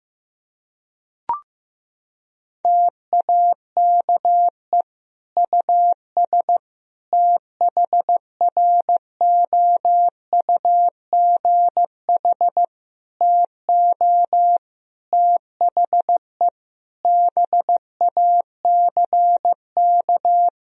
Morse code, heard as a keyed tone, about 16 wpm, 700 Hz. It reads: TAKE US THROUGH TO THE BACK
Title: Sentences from Top 1000 Words - Review Part 1 - 15wpm